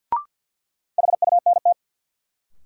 Morse code, heard as a keyed tone, about 46 wpm, 700 Hz. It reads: HUNT
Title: Sentences from Top 1600 Words - New Word Drill - 50wpm